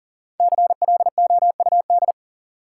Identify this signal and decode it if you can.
Morse code, heard as a keyed tone, about 39 wpm, 700 Hz. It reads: CLOUD
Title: Five Letter Words - 40wpm